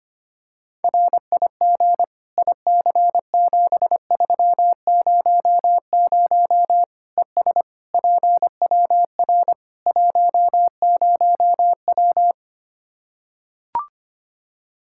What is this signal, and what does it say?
Morse code, heard as a keyed tone, about 25 wpm, 700 Hz. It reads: RIG IC7300 ES PWR 10W
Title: K1USN SST QSO Practice 25wpm